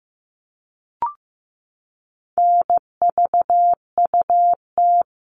Morse code, heard as a keyed tone, about 15 wpm, 700 Hz. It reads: NVUT
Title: All Cut Numbers - 4-Digit 15wpm